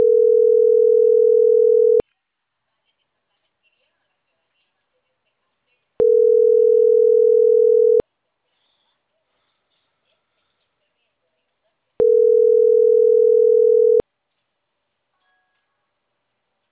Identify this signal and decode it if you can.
Ringback tone (North America), 440 + 480 Hz.